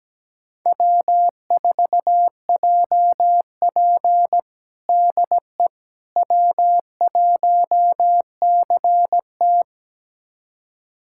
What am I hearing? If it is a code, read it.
Morse code, heard as a keyed tone, about 17 wpm, 700 Hz. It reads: W4JP DE W1CT
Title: USA Callsign DE Callsign - 4 Characters - 17wpm